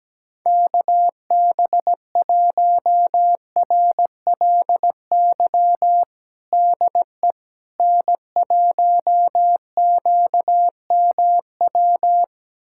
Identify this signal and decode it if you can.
Morse code, heard as a keyed tone, about 17 wpm, 700 Hz. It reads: KB1RLY DE N1QMW
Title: USA Callsign DE Callsign 17wpm